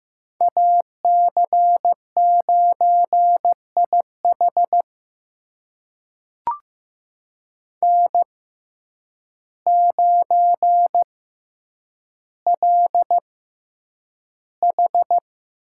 Morse code, heard as a keyed tone, about 15 wpm, spaced as for 8 wpm, 700 Hz. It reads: AC9IH N9LH
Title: USA Call Signs - 6x Character Spacing - 15wpm